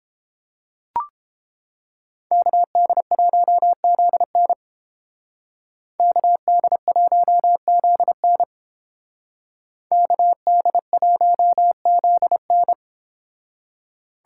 Morse code, heard as a keyed tone, about 29 wpm, 700 Hz. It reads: KD1ZN KD1ZN KD1ZN